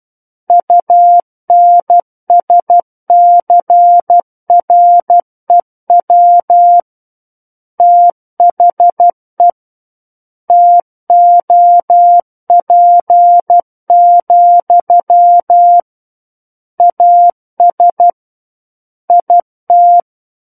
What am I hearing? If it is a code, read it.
Morse code, heard as a keyed tone, about 12 wpm, 700 Hz. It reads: UNSCREW THE TOP, AS IT
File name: Book One / 03-Chapter 3 - 12 WPM